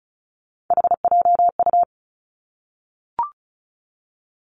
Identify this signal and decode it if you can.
Morse code, heard as a keyed tone, about 36 wpm, 700 Hz. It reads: HJU
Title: Three Challenging Letter-Number Combinations 35wpm